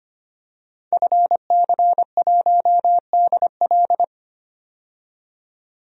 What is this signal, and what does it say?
Morse code, heard as a keyed tone, about 25 wpm, 690 Hz. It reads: FC1DL
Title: DXCC-100 Call Signs - 5 Characters - 25wpm